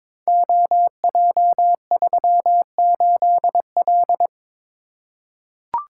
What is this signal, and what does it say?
Morse code, heard as a keyed tone, about 22 wpm, 700 Hz. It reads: OJ38L
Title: Code Groups - 5 Letter:Numbers - 22wpm